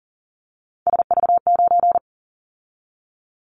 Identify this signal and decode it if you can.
Morse code, heard as a keyed tone, about 40 wpm, 700 Hz. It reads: SV9
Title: Callsign Prefixes for Top 100 DXCC - 40wpm